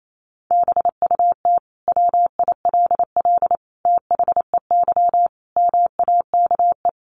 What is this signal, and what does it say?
Morse code, heard as a keyed tone, about 28 wpm, 700 Hz. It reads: BUT WILL THEY MAKE
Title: Sentences from Top 1000 Words 28wpm